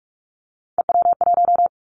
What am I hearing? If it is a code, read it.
Morse code, heard as a keyed tone, about 46 wpm, 700 Hz. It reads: EW1
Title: Callsign Prefixes for Top 50 DXCC - 45wpm